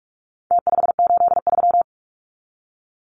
Morse code, heard as a keyed tone, about 44 wpm, 700 Hz. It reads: T583
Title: Numbers - Contest Style - 4-Digit 45wpm